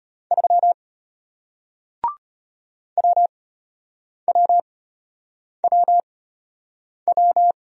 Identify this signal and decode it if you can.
Morse code, heard as a keyed tone, about 35 wpm, spaced as for 8 wpm, 700 Hz. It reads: 3 WWWW